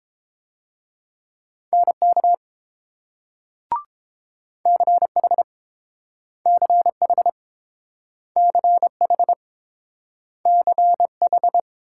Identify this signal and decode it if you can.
Morse code, heard as a keyed tone, about 30 wpm, 700 Hz. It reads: NK CH CH CH CH